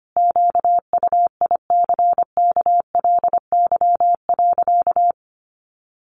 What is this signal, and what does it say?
Morse code, heard as a keyed tone, about 25 wpm, 700 Hz. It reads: QUICKLY.
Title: Sentences from Top 1300 Words 25wpm